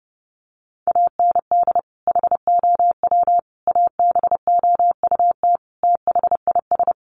Morse code, heard as a keyed tone, about 28 wpm, 700 Hz. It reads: AND HOW ABOUT THIS
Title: Sentences from Top 1200 Words 30wpm